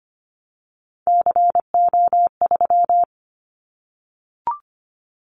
Morse code, heard as a keyed tone, about 25 wpm, 700 Hz. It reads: CO3
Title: Callsign Prefixes for Top 100 DXCC - 25wpm